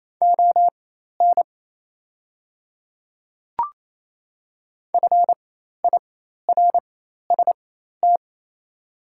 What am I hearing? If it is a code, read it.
Morse code, heard as a keyed tone, about 28 wpm, spaced as for 13 wpm, 700 Hz. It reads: ON FIRST
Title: 4x Character Spacing - Top 100 Words - 28wpm